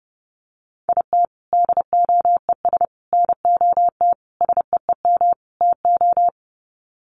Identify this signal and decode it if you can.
Morse code, heard as a keyed tone, about 30 wpm, 700 Hz. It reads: IT DOES NOT SEEM TO
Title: Sets of 5 Words Encore2 30wpm